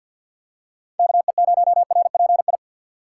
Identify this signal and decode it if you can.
Morse code, heard as a keyed tone, about 52 wpm, 700 Hz. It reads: KE0RPI
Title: USA Call Signs - 4x Character Spacing - 50wpm